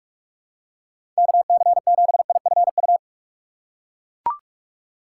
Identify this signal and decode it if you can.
Morse code, heard as a keyed tone, about 41 wpm, 700 Hz. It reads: KC7IFU
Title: USA Call Signs - 6 Characters - 2x Character Spacing - 45wpm